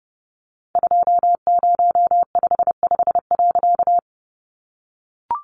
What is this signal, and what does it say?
Morse code, heard as a keyed tone, about 28 wpm, 700 Hz. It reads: 2055.
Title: A Sound Of Thunder 30wpm